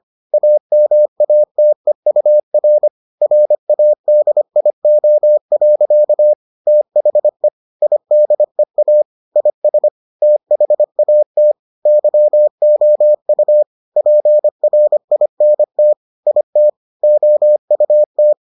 Morse code, heard as a keyed tone, about 25 wpm, 600 Hz. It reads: AMATEUR RADIO. THE IDEA IS THAT YOU PRINT IT OUT